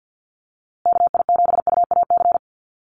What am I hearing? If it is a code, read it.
Morse code, heard as a keyed tone, about 50 wpm, 700 Hz. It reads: KI7UAC
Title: USA Call Signs Repeated 3x 50wpm